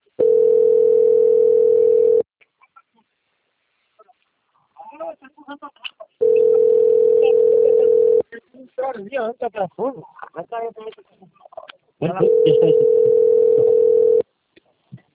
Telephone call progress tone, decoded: ringback tone

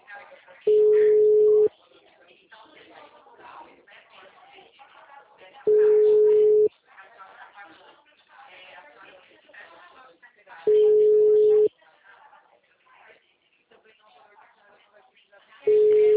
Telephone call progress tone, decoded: ringback tone

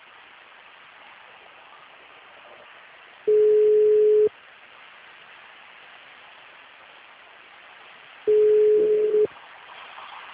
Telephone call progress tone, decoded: ringback tone

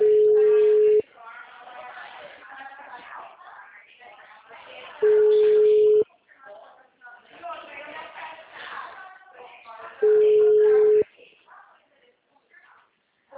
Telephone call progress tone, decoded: ringback tone

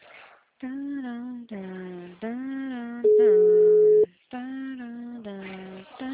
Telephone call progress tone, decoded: ringback tone

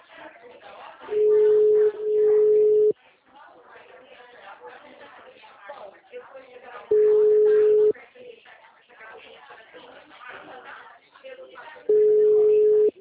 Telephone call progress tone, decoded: ringback tone

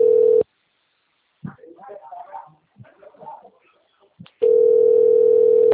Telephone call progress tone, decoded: ringback tone

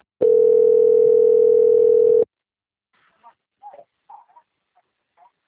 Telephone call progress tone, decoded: ringback tone

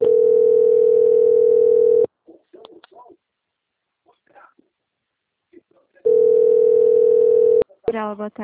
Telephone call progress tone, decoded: ringback tone

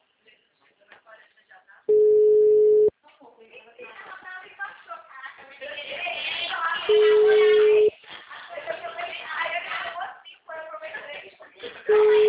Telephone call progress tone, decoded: ringback tone

